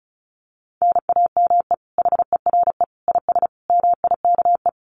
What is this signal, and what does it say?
Morse code, heard as a keyed tone, about 34 wpm, 700 Hz. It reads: NAME HERE IS MIKE
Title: Common QSO Phrases 35wpm